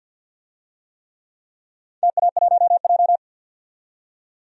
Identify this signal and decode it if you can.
Morse code, heard as a keyed tone, about 52 wpm, 700 Hz. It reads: TA1J
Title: DXCC-100 Call Signs - 4 Characters - 50wpm